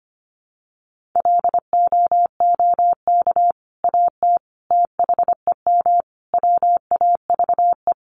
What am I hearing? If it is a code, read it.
Morse code, heard as a keyed tone, about 25 wpm, 700 Hz. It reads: LOOK AT THEM WAVE
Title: Sentences from Top 1400 Words 25wpm